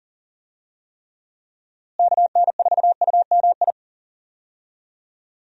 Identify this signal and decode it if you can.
Morse code, heard as a keyed tone, about 40 wpm, 700 Hz. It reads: KN4UMI
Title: USA Call Signs - 6 Characters - Repeated 3x - 40wpm